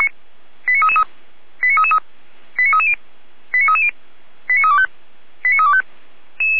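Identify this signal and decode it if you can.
Five-tone selective call (ZVEI), heard as quick runs of five tones, tones of about 70 ms. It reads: 89333 89333 89330 89330 89325 89325
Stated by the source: ZVEI five-tone selective call 89509 received by SDR, archived by Feuerwehr Orlamünde (fire brigade).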